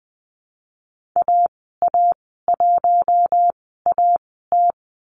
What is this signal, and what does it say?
Morse code, heard as a keyed tone, about 20 wpm, spaced as for 15 wpm, 700 Hz. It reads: AA1AT
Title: 055 Lesson 10g - Letter r - Call Signs 20wpm